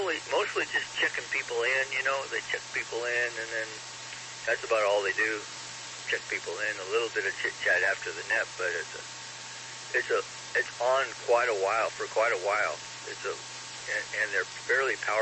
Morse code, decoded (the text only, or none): none